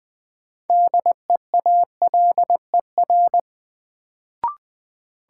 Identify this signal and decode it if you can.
Morse code, heard as a keyed tone, about 20 wpm, 700 Hz. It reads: DEALER
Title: Sentences from Top 1100 Words - New Word Drill Alt - 20wpm